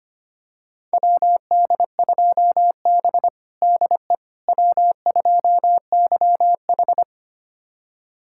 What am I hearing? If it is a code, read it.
Morse code, heard as a keyed tone, about 25 wpm, 700 Hz. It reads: WD2B DE W2YH
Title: USA Callsign DE Callsign - 4 Characters - 25wpm